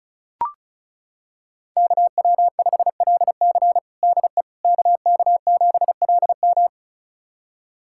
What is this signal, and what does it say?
Morse code, heard as a keyed tone, about 36 wpm, 700 Hz. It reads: KW5LC DE KK7LM
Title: USA Callsign DE Callsign - 5 Characters - 35wpm